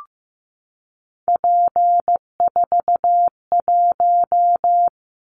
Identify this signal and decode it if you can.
Morse code, heard as a keyed tone, about 15 wpm, 700 Hz. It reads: P41
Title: Callsign Prefixes for Top 100 DXCC - 15wpm